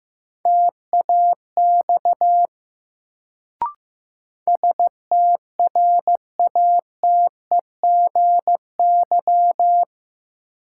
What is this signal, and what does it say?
Morse code, heard as a keyed tone, about 15 wpm, 700 Hz. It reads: TAX STRATEGY
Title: Sentences from Top 1000 Words - Review Part 5 - New Word Drill - 15wpm